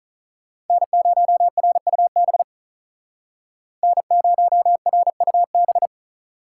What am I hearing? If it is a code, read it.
Morse code, heard as a keyed tone, about 45 wpm, 700 Hz. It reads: N0RUB N0RUB